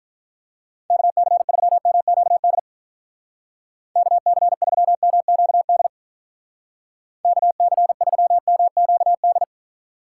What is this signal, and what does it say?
Morse code, heard as a keyed tone, about 44 wpm, 700 Hz. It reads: KC3MQD KC3MQD KC3MQD
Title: Speed Racing - USA Call Signs 35wpm